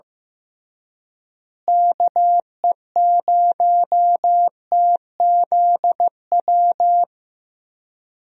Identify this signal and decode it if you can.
Morse code, heard as a keyed tone, about 15 wpm, 700 Hz. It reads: KE0TZW
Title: USA Call Signs - 6 Characters - Repeated 3x - 15wpm